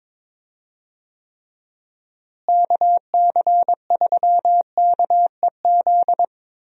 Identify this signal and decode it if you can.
Morse code, heard as a keyed tone, about 22 wpm, 700 Hz. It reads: KC3KEZ
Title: USA Call Signs - 6 Characters - Repeated 3x - 22wpm